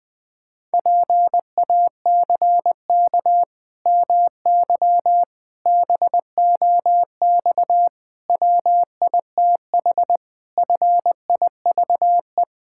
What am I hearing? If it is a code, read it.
Morse code, heard as a keyed tone, about 20 wpm, 700 Hz. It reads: PACK MY BOX WITH FIVE
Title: Pangram Challenge v2 - 20wpm